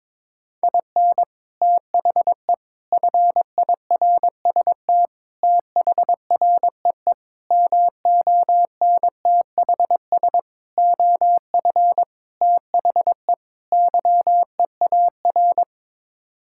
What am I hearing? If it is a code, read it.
Morse code, heard as a keyed tone, about 22 wpm, 700 Hz. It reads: IN THE FIRST THREE MONTHS OF THE YEAR
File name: practice / 22wpm / Sets of 8 Words - 22wpm